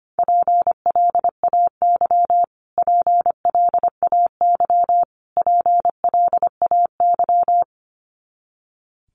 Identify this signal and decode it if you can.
Morse code, heard as a keyed tone, about 25 wpm, 700 Hz. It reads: PLAY PLAY PLAY
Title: Top 300 Words Repeated 25wpm